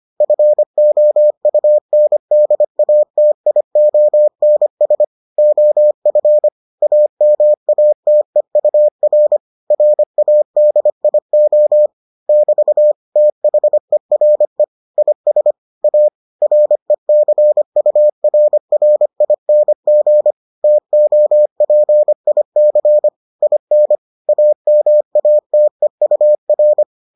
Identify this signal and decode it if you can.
Morse code, heard as a keyed tone, about 25 wpm, 600 Hz. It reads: FOUNDATIONS OF AMATEUR RADIO <BT> THERE IS A RECURRING TOPIC IN AMATEUR